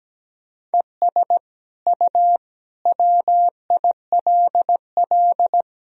Morse code, heard as a keyed tone, about 17 wpm, 700 Hz. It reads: ES U WILL
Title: Sentences from Top 400 Words and Abbreviations 17wpm